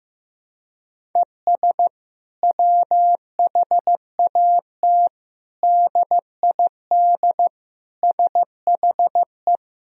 Morse code, heard as a keyed tone, about 15 wpm, 700 Hz. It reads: ES WHAT DID SHE
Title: Sentences from Top 200 Words and Abbreviations 15wpm